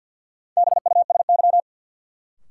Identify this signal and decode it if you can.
Morse code, heard as a keyed tone, about 47 wpm, 700 Hz. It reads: BUSY
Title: Sentences from Top 1000 Words - Review Part 2 - New Word Drill - 50wpm